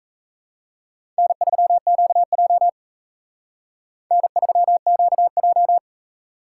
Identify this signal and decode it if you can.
Morse code, heard as a keyed tone, about 35 wpm, 700 Hz. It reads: N3QJ N3QJ